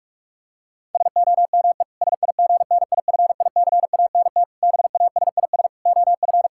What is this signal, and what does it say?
Morse code, heard as a keyed tone, about 46 wpm, 700 Hz. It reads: SOME SIGNIFICANT BASIS OF
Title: Sentences from Top 1600 Words 45wpm